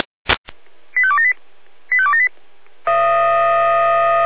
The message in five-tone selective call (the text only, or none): 85278 85278